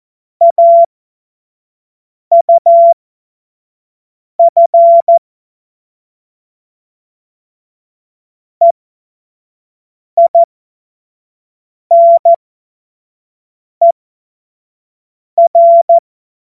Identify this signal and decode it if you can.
Morse code, heard as a keyed tone, about 14 wpm, spaced as for 5 wpm, 680 Hz. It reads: AUF EINER
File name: Witz-23-September-2025-25-F70